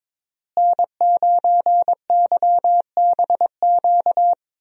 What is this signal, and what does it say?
Morse code, heard as a keyed tone, about 22 wpm, 700 Hz. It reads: N9YBQ
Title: USA Call Signs - 5 Characters - Repeated 3x - 22wpm